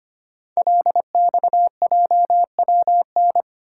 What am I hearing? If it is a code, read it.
Morse code, heard as a keyed tone, about 25 wpm, 690 Hz. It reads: LXJWN
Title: Code Groups - 5 Letter:Numbers - 25wpm